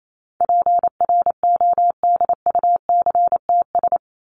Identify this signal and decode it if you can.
Morse code, heard as a keyed tone, about 28 wpm, 700 Hz. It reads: PRODUCTS
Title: ES:S Suffix - Top 100 Words with Base Words 28wpm